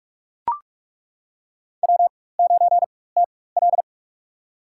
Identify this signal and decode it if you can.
Morse code, heard as a keyed tone, about 42 wpm, spaced as for 20 wpm, 700 Hz. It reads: W9TL
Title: USA Call Signs - 4 Characters - 4x Character Spacing - 45wpm